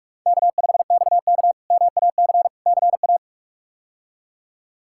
Morse code, heard as a keyed tone, about 42 wpm, 700 Hz. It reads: K5XK MAC CA
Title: K1USN SST QSO Practice - 45wpm